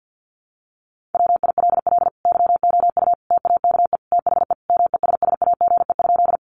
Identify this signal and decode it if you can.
Morse code, heard as a keyed tone, about 50 wpm, 700 Hz. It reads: WILL YOU TAKE THE MESSAGE?
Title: Sentences from Top 1100 Words 50wpm